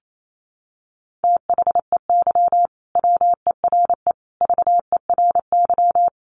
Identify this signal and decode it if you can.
Morse code, heard as a keyed tone, about 28 wpm, 700 Hz. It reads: THEY WERE VERY
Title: Sentences from Top 1100 Words 28wpm